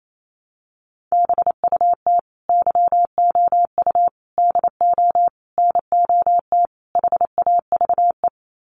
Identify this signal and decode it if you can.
Morse code, heard as a keyed tone, about 28 wpm, 700 Hz. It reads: BUT YOU DO NOT HAVE
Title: Sets of 5 Words Encore 28wpm